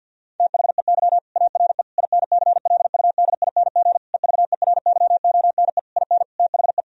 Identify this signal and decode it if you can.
Morse code, heard as a keyed tone, about 50 wpm, 700 Hz. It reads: THEY ARE INCLUDING EVERYONE IN THE
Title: Sentences from Top 1000 Words - Review Part 4 - 50wpm